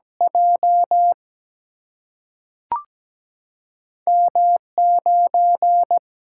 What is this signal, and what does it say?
Morse code, heard as a keyed tone, about 17 wpm, 700 Hz. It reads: J M 9
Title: Single-Letter Single-Number 17wpm